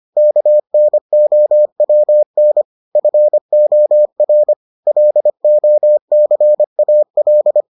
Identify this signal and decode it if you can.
Morse code, heard as a keyed tone, about 25 wpm, 600 Hz. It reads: KNOWN FOR LOCAL